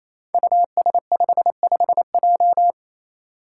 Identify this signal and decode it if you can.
Morse code, heard as a keyed tone, about 28 wpm, 700 Hz. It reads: US55J
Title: Five Challenging Letter-Number Combinations 28wpm